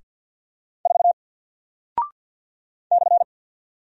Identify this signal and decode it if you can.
Morse code, heard as a keyed tone, about 48 wpm, 700 Hz. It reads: 4 /
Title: Single Character (Letter:Number:Punctuation:Prosign) Alt 50wpm